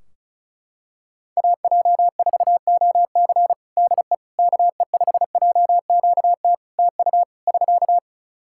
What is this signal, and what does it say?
Morse code, heard as a keyed tone, about 36 wpm, 700 Hz. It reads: AJ4OC DE KE5JQT TU <SK>
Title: K1USN SST QSO Practice 35wpm